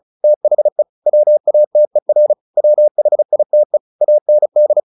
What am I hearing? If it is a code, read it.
Morse code, heard as a keyed tone, about 34 wpm, 600 Hz. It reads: THE WATER WHITE AND
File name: the old man and the sea / the old man and the sea 1 35wpm